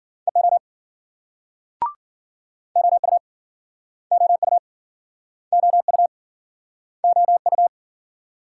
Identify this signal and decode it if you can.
Morse code, heard as a keyed tone, about 53 wpm, 700 Hz. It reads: EO OU OU OU OU